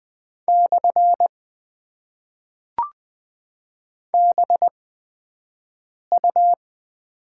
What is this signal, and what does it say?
Morse code, heard as a keyed tone, about 20 wpm, spaced as for 5 wpm, 700 Hz. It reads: / BU
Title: Top 90 QSO Elements - 8x Character Spacing - 20wpm